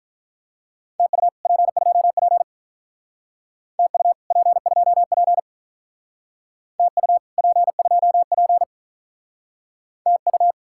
Morse code, heard as a keyed tone, about 45 wpm, 700 Hz. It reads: TU P2P TU P2P TU P2P TU